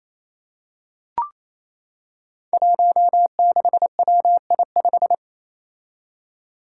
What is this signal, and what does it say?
Morse code, heard as a keyed tone, about 27 wpm, 700 Hz. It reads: 16WI5
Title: Five Challenging Letter-Number Combinations 28wpm